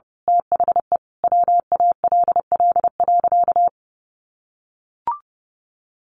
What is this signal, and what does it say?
Morse code, heard as a keyed tone, about 30 wpm, 700 Hz. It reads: THE WALL.
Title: Sentences from Top 1100 Words 30wpm